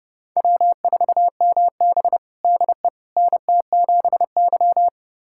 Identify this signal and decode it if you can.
Morse code, heard as a keyed tone, about 30 wpm, 700 Hz. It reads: W4MB DE NT7Y
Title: USA Callsign DE Callsign - 4 Characters - 30wpm